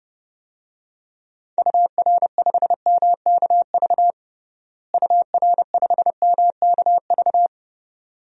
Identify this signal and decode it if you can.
Morse code, heard as a keyed tone, about 29 wpm, 700 Hz. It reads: UR5MKV UR5MKV
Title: DXCC-100 Call Signs - 6 Characters - Repeated 2x - 30wpm